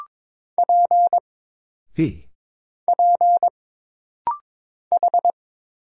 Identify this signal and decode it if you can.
Morse code, heard as a keyed tone, about 22 wpm, 700 Hz. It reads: P P H
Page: This rapid-fire timing is geared to push your ICR (Instant Character Recognition) skills.